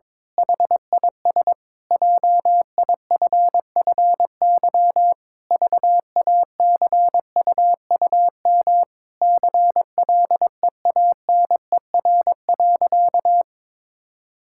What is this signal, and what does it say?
Morse code, heard as a keyed tone, about 22 wpm, 700 Hz. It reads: HIS JIFFY VACUUM CLEANER.
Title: Pangram Challenge v2 - 22wpm